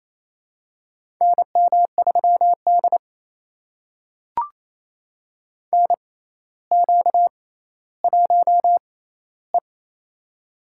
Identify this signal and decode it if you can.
Morse code, heard as a keyed tone, about 28 wpm, spaced as for 14 wpm, 700 Hz. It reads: NM3D NQ1E
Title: USA Call Signs - 4 Characters - 6x Character Spacing - 28wpm